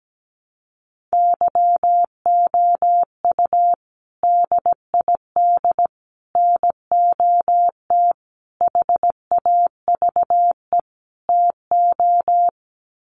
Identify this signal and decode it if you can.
Morse code, heard as a keyed tone, about 17 wpm, 700 Hz. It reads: YOU DID NOT HAVE TO
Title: Sets of 5 Words Encore 17wpm